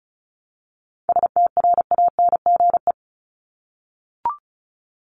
Morse code, heard as a keyed tone, about 34 wpm, 700 Hz. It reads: STRANGE
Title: Top 1501-1750 Words 35wpm